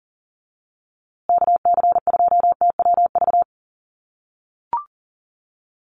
Morse code, heard as a keyed tone, about 40 wpm, 700 Hz. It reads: KC2TWV
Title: USA Call Signs Repeated 3x 40wpm